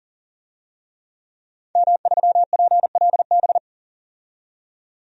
Morse code, heard as a keyed tone, about 40 wpm, 700 Hz. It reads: M3PLB
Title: DXCC-100 Call Signs - 5 Characters - 40wpm